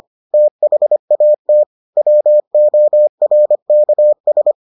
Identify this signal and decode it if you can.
Morse code, heard as a keyed tone, about 25 wpm, 600 Hz. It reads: THAT WORKS